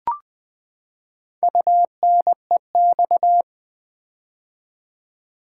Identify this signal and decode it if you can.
Morse code, heard as a keyed tone, about 20 wpm, 700 Hz. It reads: UNEX
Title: Build A Word - 4 Letters At A Time - 10-Letter Words - 20wpm